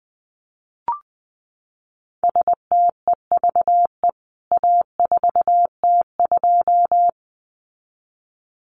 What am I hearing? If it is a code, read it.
Morse code, heard as a keyed tone, about 20 wpm, 700 Hz. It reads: STEVE A4T2
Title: CWT Exchange 20wpm